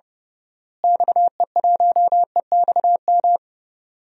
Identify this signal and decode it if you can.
Morse code, heard as a keyed tone, about 30 wpm, 700 Hz. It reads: XE1EXM
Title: DXCC-100 Call Signs - 6 Characters - Repeated 3x - 30wpm